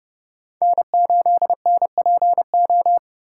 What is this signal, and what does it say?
Morse code, heard as a keyed tone, about 29 wpm, 700 Hz. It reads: N8NPO